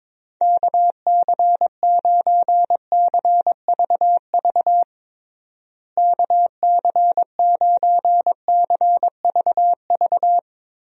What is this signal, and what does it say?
Morse code, heard as a keyed tone, about 22 wpm, 700 Hz. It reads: KC9CVV KC9CVV
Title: USA Call Signs - 6 Characters - Repeated 3x - 22wpm